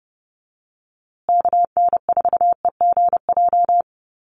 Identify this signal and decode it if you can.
Morse code, heard as a keyed tone, about 30 wpm, 700 Hz. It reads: KN4EGJ
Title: USA Call Signs Repeated 3x 30wpm